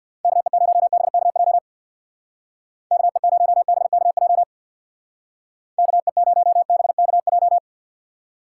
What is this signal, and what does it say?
Morse code, heard as a keyed tone, about 59 wpm, 700 Hz. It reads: KE0BKJ KE0BKJ KE0BKJ
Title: Speed Racing - USA 2x3 Call Signs 50wpm